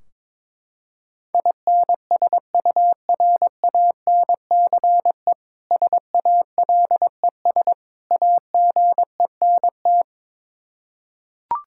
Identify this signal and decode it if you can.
Morse code, heard as a keyed tone, about 22 wpm, 700 Hz. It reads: INSURANCE SALES AGENT
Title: Practice QSO - Exchange 3 - More Info - 22wpm